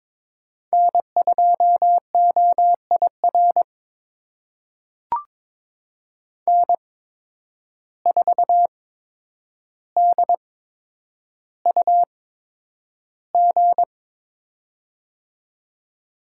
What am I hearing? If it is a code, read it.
Morse code, heard as a keyed tone, about 22 wpm, spaced as for 9 wpm, 700 Hz. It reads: N2OIR N4DUG
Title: USA Call Signs - 5 Characters - 8x Character Spacing - 22wpm